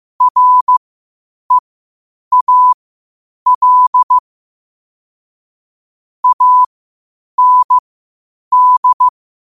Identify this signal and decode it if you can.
Morse code, heard as a keyed tone, about 15 wpm, spaced as for 8 wpm, 1000 Hz. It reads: REAL AND